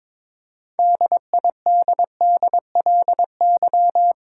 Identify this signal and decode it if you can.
Morse code, heard as a keyed tone, about 22 wpm, 700 Hz. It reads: DIDDLY
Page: morse-code-ninja-practice-22wpm